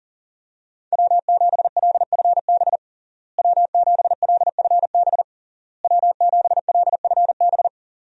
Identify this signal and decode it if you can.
Morse code, heard as a keyed tone, about 37 wpm, 700 Hz. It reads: W7LFB W7LFB W7LFB
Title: USA Call Signs - 5 Characters - Repeated 3x - 40wpm